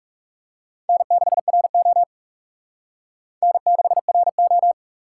Morse code, heard as a keyed tone, about 39 wpm, 700 Hz. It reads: N6RO N6RO